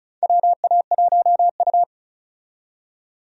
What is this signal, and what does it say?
Morse code, heard as a keyed tone, about 36 wpm, 700 Hz. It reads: WA1U
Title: USA Call Signs - 4 Characters - 35wpm